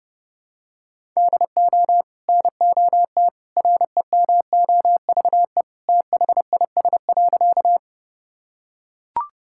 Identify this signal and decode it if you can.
Morse code, heard as a keyed tone, about 32 wpm, 700 Hz. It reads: DO NOT REMOVE THIS.
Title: Sentences from Top 1000 Words - Review Part 5 - 30wpm